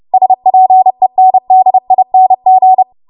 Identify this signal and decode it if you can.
Morse code, heard as a keyed tone, about 30 wpm, 750 Hz. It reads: SPENDING
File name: Morse / 30WPM